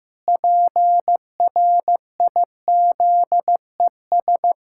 Morse code, heard as a keyed tone, about 15 wpm, 700 Hz. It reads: PRIZES
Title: Pangram Challenge v2 - 15wpm